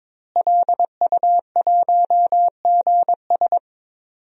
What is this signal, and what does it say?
Morse code, heard as a keyed tone, about 22 wpm, 700 Hz. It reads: LU1GS
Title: DXCC-100 Call Signs - 5 Characters - Repeated 3x - 22wpm